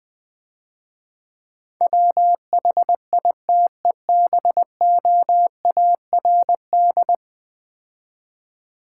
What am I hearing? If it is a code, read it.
Morse code, heard as a keyed tone, about 20 wpm, 700 Hz. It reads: WHITEBOARD
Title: Easy Words - Speed-Racing with Hints - 20wpm